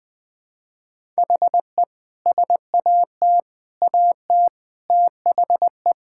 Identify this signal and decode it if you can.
Morse code, heard as a keyed tone, about 20 wpm, 700 Hz. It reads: HE SAT AT THE
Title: Sentences with 5-Word Phrases - 20wpm